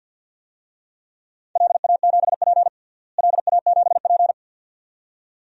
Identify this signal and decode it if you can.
Morse code, heard as a keyed tone, about 52 wpm, 700 Hz. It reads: LA7P LA7P
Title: DXCC-100 Call Signs - 4 Characters - Repeated 2x - 50wpm